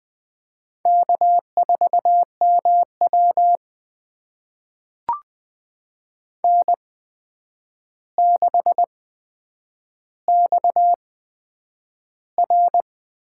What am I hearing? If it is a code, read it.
Morse code, heard as a keyed tone, about 20 wpm, spaced as for 9 wpm, 700 Hz. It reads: K4MW N6XR
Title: USA Call Signs - 8x Character Spacing - 20wpm